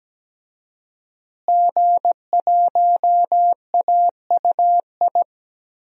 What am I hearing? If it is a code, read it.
Morse code, heard as a keyed tone, about 17 wpm, 700 Hz. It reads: G1AUI
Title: DXCC-100 Call Signs - 5 Characters - Repeated 2x - 17wpm